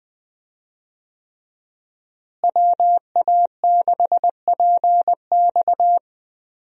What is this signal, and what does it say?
Morse code, heard as a keyed tone, about 21 wpm, 700 Hz. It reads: WA6PX
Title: USA Call Signs - 5 Characters - Repeated 3x - 20wpm